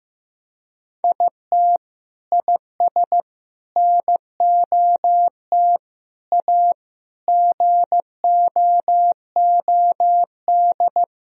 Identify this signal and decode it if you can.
Morse code, heard as a keyed tone, about 14 wpm, 700 Hz. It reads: IT IS NOT A GOOD
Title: Sentences from Top 1000 Words - Review Part 2 - 15wpm